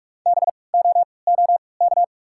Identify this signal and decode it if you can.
Morse code, heard as a keyed tone, about 48 wpm, spaced as for 26 wpm, 690 Hz. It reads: BOOK
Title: 3x Character Spacing - Top 300 Words - 45wpm